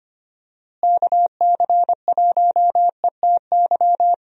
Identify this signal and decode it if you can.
Morse code, heard as a keyed tone, about 25 wpm, 700 Hz. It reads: KC1ETY
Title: USA Call Signs - 6 Characters - Repeated 3x - 25wpm